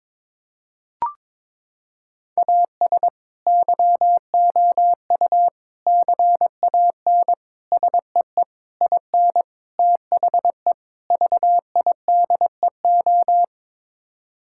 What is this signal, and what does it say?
Morse code, heard as a keyed tone, about 22 wpm, 700 Hz. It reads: AS YOU CAN SEE IN THE VIDEO
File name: practice / 22wpm / Sets of 7 Words - 22wpm